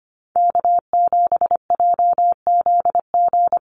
Morse code, heard as a keyed tone, about 25 wpm, 700 Hz. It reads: K7JZG
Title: USA Call Signs Repeated 3x 25wpm